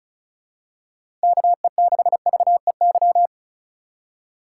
Morse code, heard as a keyed tone, about 34 wpm, 700 Hz. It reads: KE6VEY
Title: USA Call Signs - 2x Character Spacing - 35wpm